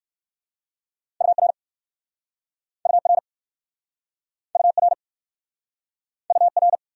Morse code, heard as a keyed tone, about 60 wpm, 700 Hz. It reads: UR UR UR UR